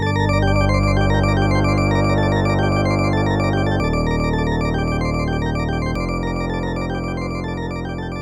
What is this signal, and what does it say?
There is no code.